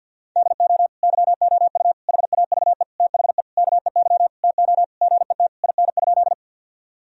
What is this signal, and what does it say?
Morse code, heard as a keyed tone, about 50 wpm, 700 Hz. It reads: DO YOU HAVE THE KEY TO GET IN?
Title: Sentences from Top 1000 Words - Review Part 4 - 50wpm